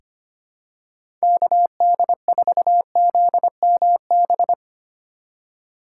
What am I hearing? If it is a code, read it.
Morse code, heard as a keyed tone, about 25 wpm, 700 Hz. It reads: KD4ZMB
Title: USA Call Signs - 4x Character Spacing - 25wpm